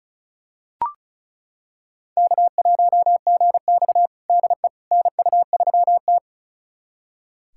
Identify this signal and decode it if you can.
Morse code, heard as a keyed tone, about 36 wpm, 700 Hz. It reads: K1GX DE NU3T
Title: USA Callsign DE Callsign - 4 Characters - 35wpm